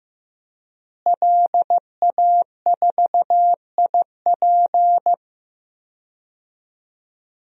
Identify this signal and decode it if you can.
Morse code, heard as a keyed tone, about 15 wpm, 700 Hz. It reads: LA4IP
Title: DXCC-100 Call Signs - 5 Characters - Repeated 2x - 15wpm